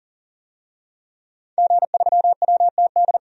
Morse code, heard as a keyed tone, about 40 wpm, 700 Hz. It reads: G3WTD
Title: DXCC-100 Call Signs - 5 Characters - 40wpm